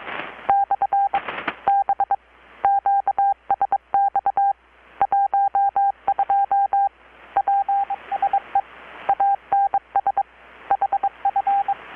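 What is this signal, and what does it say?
Morse code, heard as a keyed tone, about 23 wpm, 800 Hz, with radio noise. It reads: / B QSX 12 PSE ANS HF
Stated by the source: Morse from KPH's Night of Nights XX maritime radiotelegraphy event, received in Chiapas, Mexico about 3,800 km away on a Kenwood R-600 with a grounded 10 m long-wire antenna.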